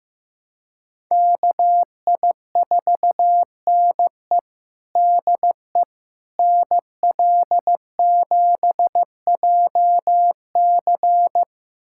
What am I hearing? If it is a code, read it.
Morse code, heard as a keyed tone, about 15 wpm, 700 Hz. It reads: KI4NE DE NL7JC
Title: USA Callsign DE Callsign 15wpm